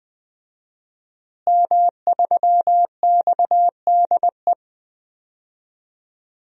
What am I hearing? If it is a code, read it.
Morse code, heard as a keyed tone, about 20 wpm, 700 Hz. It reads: M3XDE
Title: DXCC-100 Call Signs - 5 Characters - Repeated 2x - 20wpm